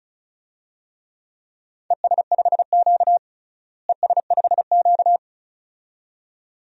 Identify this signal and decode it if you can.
Morse code, heard as a keyed tone, about 36 wpm, 700 Hz. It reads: ES5Q ES5Q
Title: DXCC-100 Call Signs - 4 Characters - Repeated 2x - 35wpm